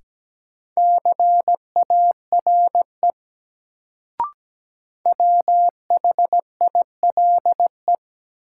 Morse code, heard as a keyed tone, about 17 wpm, 700 Hz. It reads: CARE WHILE
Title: Sets of 3 Words - New Word Drill - 17wpm